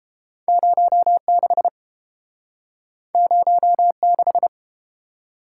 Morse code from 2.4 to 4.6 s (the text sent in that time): 06